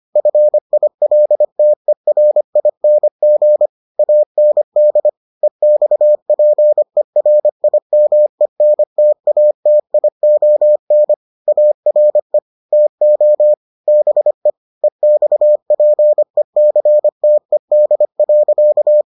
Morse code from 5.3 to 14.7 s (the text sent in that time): EXPERIMENTATION ARE TO BE